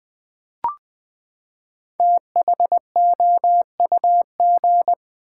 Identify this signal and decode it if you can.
Morse code, heard as a keyed tone, about 20 wpm, 700 Hz. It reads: THOUG
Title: Build A Word - 5 Letters At A Time - 10-Letter Words - 20wpm